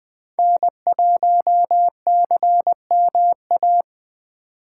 Morse code, heard as a keyed tone, about 21 wpm, 700 Hz. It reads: N1CMA
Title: USA Call Signs - 5 Characters - 6x Character Spacing - 20wpm